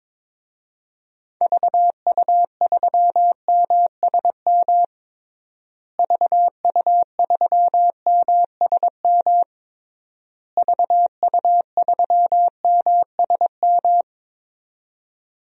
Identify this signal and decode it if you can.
Morse code, heard as a keyed tone, about 22 wpm, 700 Hz. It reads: VU3MSM VU3MSM VU3MSM
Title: DXCC-100 Call Signs - 6 Characters - Repeated 3x - 22wpm